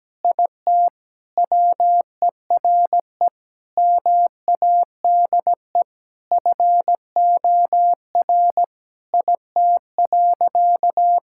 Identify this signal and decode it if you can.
Morse code, heard as a keyed tone, about 17 wpm, 700 Hz. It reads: IT WERE MADE FOR IT.
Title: Sentences from Top 1600 Words 17wpm